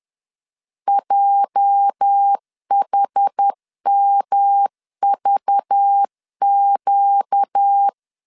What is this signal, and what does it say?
Morse code, heard as a keyed tone, about 10 wpm, 800 Hz. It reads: JHMVQ
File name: LEZIONI / 9 28 10wpm 2842